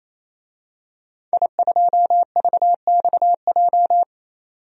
Morse code, heard as a keyed tone, about 28 wpm, 700 Hz. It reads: I2VXJ
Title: DXCC-100 Call Signs - 5 Characters - 28wpm